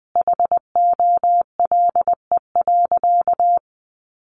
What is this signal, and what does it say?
Morse code, heard as a keyed tone, about 20 wpm, 700 Hz. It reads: HOLE.
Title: Easy Phrases 20wpm